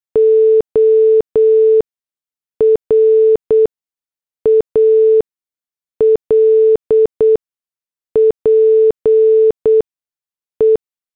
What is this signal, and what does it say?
Morse code, heard as a keyed tone, about 8 wpm, spaced as for 6 wpm, 440 Hz. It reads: ORAL PE